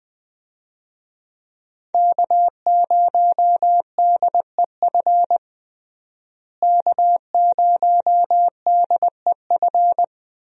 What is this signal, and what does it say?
Morse code, heard as a keyed tone, about 20 wpm, 700 Hz. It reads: K0DEF K0DEF
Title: USA Call Signs - 5 Characters - Repeated 2x - 20wpm